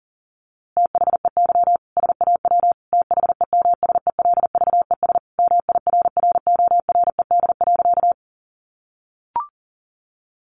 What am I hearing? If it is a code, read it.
Morse code, heard as a keyed tone, about 40 wpm, 700 Hz. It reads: THEY SAW THEMSELVES MIRRORED.
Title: A Sound Of Thunder 40wpm